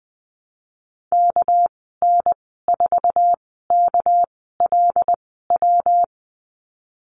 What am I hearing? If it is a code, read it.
Morse code, heard as a keyed tone, about 19 wpm, spaced as for 14 wpm, 700 Hz. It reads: KN4KLW
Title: 219 Lesson 35g - Number 0 - Call Signs 20wpm